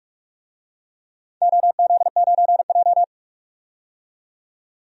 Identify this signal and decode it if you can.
Morse code, heard as a keyed tone, about 41 wpm, 700 Hz. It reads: OZ9J
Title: DXCC-100 Call Signs - 4 Characters - 45wpm